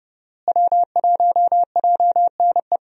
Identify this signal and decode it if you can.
Morse code, heard as a keyed tone, about 32 wpm, 700 Hz. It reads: W1JNE
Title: USA Call Signs - 5 Characters - Repeated 3x - 30wpm